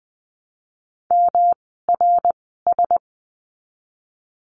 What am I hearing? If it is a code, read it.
Morse code, heard as a keyed tone, about 20 wpm, spaced as for 15 wpm, 700 Hz. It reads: MRS
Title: 108 Lesson 18f - Letter m - Words 20wpm